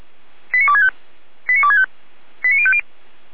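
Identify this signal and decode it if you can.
Five-tone selective call (ZVEI), heard as quick runs of five tones, tones of about 70 ms. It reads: 89265 89265 89050
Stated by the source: ZVEI five-tone selective call received by SDR, address 89240, from a server of Feuerwehr Orlamünde (fire brigade).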